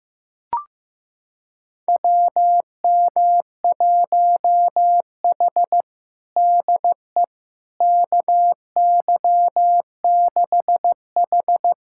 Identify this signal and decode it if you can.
Morse code, heard as a keyed tone, about 15 wpm, 700 Hz. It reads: WM1H DE KY6H
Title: USA Callsign DE Callsign - 4 Characters - 15wpm